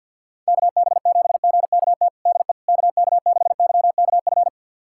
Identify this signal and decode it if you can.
Morse code, heard as a keyed tone, about 52 wpm, 700 Hz. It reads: KB7GKT DE KK6YKF
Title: USA Callsign DE Callsign 50wpm